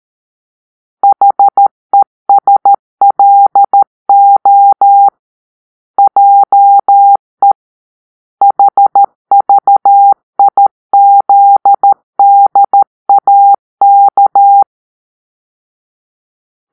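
Morse code, heard as a keyed tone, about 13 wpm, 800 Hz. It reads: HESLO JE HVIZDAK
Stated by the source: Morse-code cipher made by a player team for the Kachna puzzle game.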